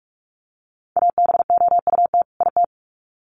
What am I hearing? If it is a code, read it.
Morse code, heard as a keyed tone, about 44 wpm, 700 Hz. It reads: ABOUT IT
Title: Sets of 2 Words 45wpm